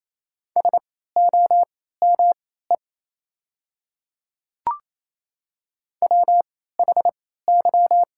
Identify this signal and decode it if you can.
Morse code, heard as a keyed tone, about 28 wpm, spaced as for 16 wpm, 700 Hz. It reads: SOME WHY
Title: 3x Character Spacing - Top 200 Words - 28wpm